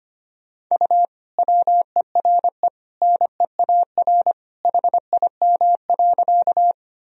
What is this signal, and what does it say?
Morse code, heard as a keyed tone, about 25 wpm, 700 Hz. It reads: U WERE NEAR HIM.
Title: Sentences from Top 200 Words and Abbreviations 25wpm